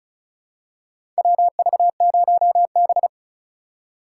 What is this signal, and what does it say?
Morse code, heard as a keyed tone, about 36 wpm, 700 Hz. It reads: WV0B
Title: USA Call Signs - 4 Characters - 3x Character Spacing - 35wpm